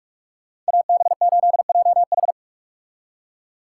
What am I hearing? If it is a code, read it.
Morse code, heard as a keyed tone, about 59 wpm, 690 Hz. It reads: AB8JH